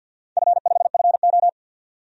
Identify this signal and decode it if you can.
Morse code, heard as a keyed tone, about 53 wpm, 700 Hz. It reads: F5FO
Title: DXCC-100 Call Signs - 4 Characters - 50wpm